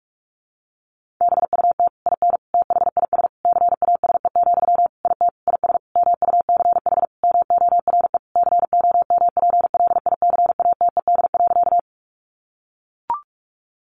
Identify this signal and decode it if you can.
Morse code, heard as a keyed tone, about 44 wpm, 700 Hz. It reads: BUT IN THIS CASE, IT IS MUCH MORE COMPLICATED.
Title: Sentences from Top 1200 Words 45wpm